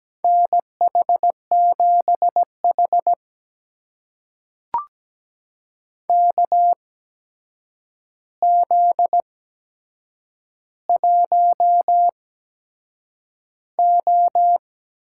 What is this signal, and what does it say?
Morse code, heard as a keyed tone, about 17 wpm, spaced as for 7 wpm, 700 Hz. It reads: NH7H KZ1O
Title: USA Call Signs - 4 Characters - 8x Character Spacing - 17wpm